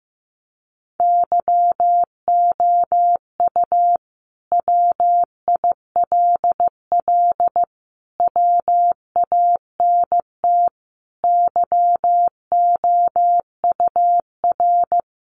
Morse code, heard as a keyed tone, about 15 wpm, 700 Hz. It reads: YOU WILL WANT YOUR
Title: Sentences from Top 1300 Words 15wpm